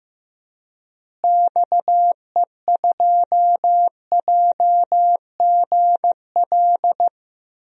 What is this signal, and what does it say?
Morse code, heard as a keyed tone, about 15 wpm, 700 Hz. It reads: XE2JGL
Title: DXCC-100 Call Signs - 6 Characters - Repeated 3x - 15wpm